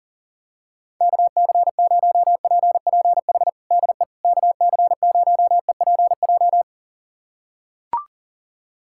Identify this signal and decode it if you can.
Morse code, heard as a keyed tone, about 40 wpm, 700 Hz. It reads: KC0PPH DE KC0EPJ
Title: USA Callsign DE Callsign 40wpm